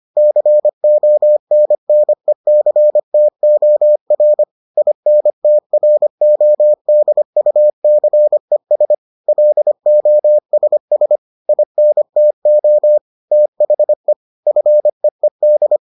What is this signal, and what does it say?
Morse code, heard as a keyed tone, about 25 wpm, 600 Hz. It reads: CONNECTOR INTRODUCES LOSS INTO THE FEED